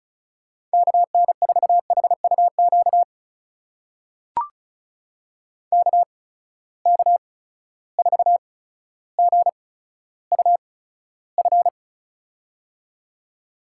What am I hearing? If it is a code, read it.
Morse code, heard as a keyed tone, about 36 wpm, spaced as for 15 wpm, 700 Hz. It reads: KN4HUQ KK4GUF
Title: USA Call Signs - 6 Characters - 8x Character Spacing - 35wpm